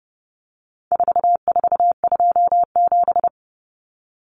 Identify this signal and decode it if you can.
Morse code, heard as a keyed tone, about 30 wpm, 700 Hz. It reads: 4427
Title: Numbers - 4-Digit 30wpm